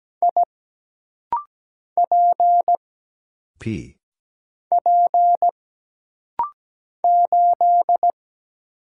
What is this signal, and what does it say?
Morse code, heard as a keyed tone, about 17 wpm, 700 Hz. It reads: I P P 8
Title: Single Letter-Number Alt - Rapid-Fire 17wpm